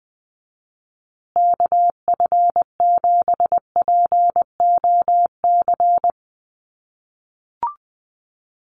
Morse code, heard as a keyed tone, about 20 wpm, 700 Hz. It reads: KF7POC
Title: USA Call Signs Repeated 2x 20wpm